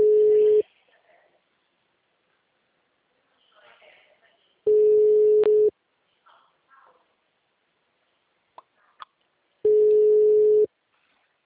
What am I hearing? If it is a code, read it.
Ringback tone (Europe, 425 Hz), with speech.